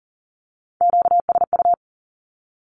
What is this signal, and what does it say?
Morse code, heard as a keyed tone, about 42 wpm, 700 Hz. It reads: QSU
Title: Q-Codes with Spoken Meaning - 40wpm